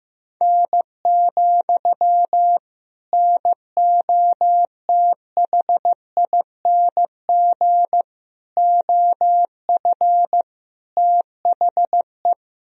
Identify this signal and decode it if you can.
Morse code, heard as a keyed tone, about 15 wpm, 700 Hz. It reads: N, NOTHING OF THE
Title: Sentences from Top 500 Words and Abbreviations 15wpm